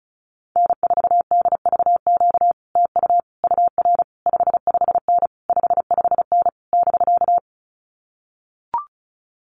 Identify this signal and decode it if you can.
Morse code, heard as a keyed tone, about 34 wpm, 700 Hz. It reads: N4DVQ TU UR 55N 55N <BK>
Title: Common QSO Phrases Encore 35wpm